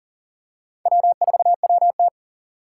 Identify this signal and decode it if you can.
Morse code, heard as a keyed tone, about 40 wpm, 700 Hz. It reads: W4WT
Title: USA Call Signs - 4 Characters - 6x Character Spacing - 40wpm